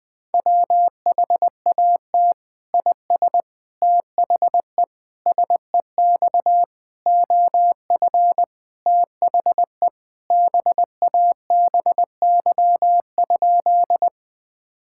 Morse code, heard as a keyed tone, about 20 wpm, 700 Hz. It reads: WHAT IS THE SEX OF THE BABY?
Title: Sentences from Top 1600 Words 20wpm